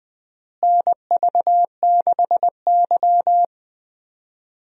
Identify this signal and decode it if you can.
Morse code, heard as a keyed tone, about 20 wpm, 700 Hz. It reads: NV6Y
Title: USA Call Signs - 4 Characters - Repeated 3x - 20wpm